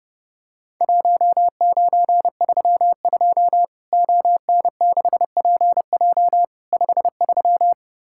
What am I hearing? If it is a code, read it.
Morse code, heard as a keyed tone, about 32 wpm, 700 Hz. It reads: 1932 ON6PJ 53
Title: Worked All Europe DX Contest - QTC Exchange - 30wpm